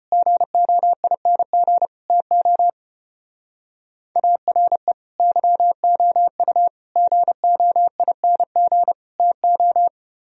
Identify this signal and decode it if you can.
Morse code, heard as a keyed tone, about 30 wpm, 700 Hz. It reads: GOING TO ARE YOU GOING TO
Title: Speed-Racing - Sets of 4 Words 25wpm